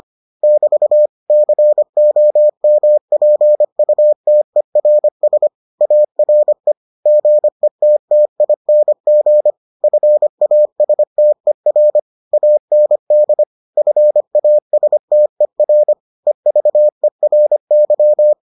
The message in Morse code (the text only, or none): <BT> COMPUTERS ARE GETTING FASTER AND FASTER EVERY